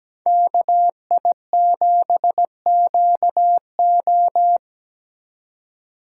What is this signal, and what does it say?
Morse code, heard as a keyed tone, about 20 wpm, 700 Hz. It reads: KI7QO
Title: Speed Racing - USA 2x2 Call Signs 17wpm